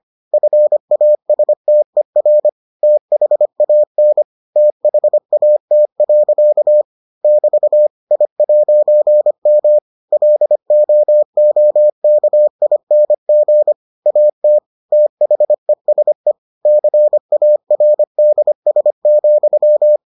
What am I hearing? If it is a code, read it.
Morse code, heard as a keyed tone, about 25 wpm, 600 Hz. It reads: FASTER THAN THAT. <BT> I'M LOOKING AT THESE CARDS,